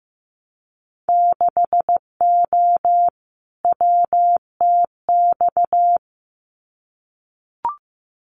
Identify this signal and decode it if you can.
Morse code, heard as a keyed tone, about 15 wpm, 700 Hz. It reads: 6O WTX
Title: Winter Field Day Exchanges Encore 15wpm